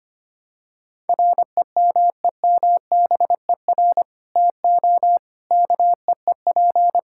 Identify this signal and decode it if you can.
Morse code, heard as a keyed tone, about 25 wpm, 700 Hz. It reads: REMEMBER TO KEEP
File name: practice / 25wpm / Sentences with 3-Word Phrases - 25wpm